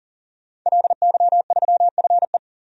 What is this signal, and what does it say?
Morse code, heard as a keyed tone, about 40 wpm, 700 Hz. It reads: LY3FE
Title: DXCC-100 Call Signs - 5 Characters - Repeated 3x - 40wpm